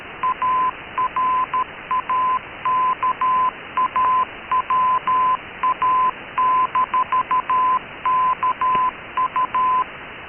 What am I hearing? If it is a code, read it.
Morse code, heard as a keyed tone, about 13 wpm, 1020 Hz. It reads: ARAKAWA-KU
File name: KKS-1 cw 21jul2014 1425z